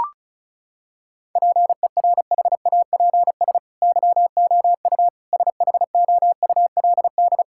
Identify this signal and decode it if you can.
Morse code, heard as a keyed tone, about 34 wpm, 700 Hz. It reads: PERHAPS YOU SHOULD
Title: Sentences from Top 1000 Words - Review Part 6 - 35wpm